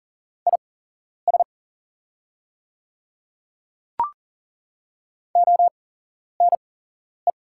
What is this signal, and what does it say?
Morse code, heard as a keyed tone, about 39 wpm, spaced as for 11 wpm, 700 Hz. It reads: IS ONE
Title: 8x Character Spacing - Top 200 Words - 40wpm